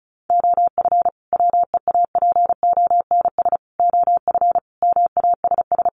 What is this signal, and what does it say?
Morse code, heard as a keyed tone, about 34 wpm, 700 Hz. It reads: OF WEAPONS OF MASS
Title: Sets of 5 Words Encore 35wpm